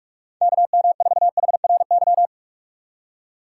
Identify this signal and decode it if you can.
Morse code, heard as a keyed tone, about 42 wpm, 700 Hz. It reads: KM4HRY
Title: USA Call Signs - 6x Character Spacing - 45wpm